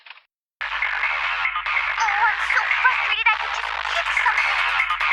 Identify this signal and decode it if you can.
Emergency Alert System (SAME) tones, heard as harsh buzzing bursts. It is heard from 0.7 to 4.5 s.